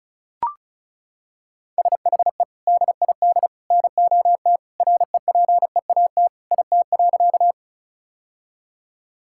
Morse code, heard as a keyed tone, about 34 wpm, 700 Hz. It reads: SHE DID NOT REPEAT IT.
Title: Sentences from Top 1600 Words 35wpm